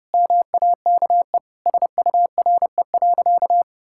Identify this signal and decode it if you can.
Morse code, heard as a keyed tone, about 30 wpm, 700 Hz. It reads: MAKE SURE.
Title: Sentences from Top 1000 Words - Review Part 5 - 30wpm